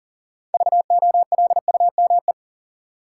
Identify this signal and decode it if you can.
Morse code, heard as a keyed tone, about 40 wpm, 700 Hz. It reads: VOLUME
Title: Six Letter Words - 40wpm